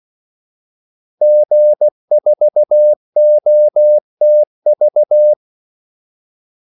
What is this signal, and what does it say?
Morse code, heard as a keyed tone, about 16 wpm, 600 Hz. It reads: G4OTV
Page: Kaldesignaler 16 wpm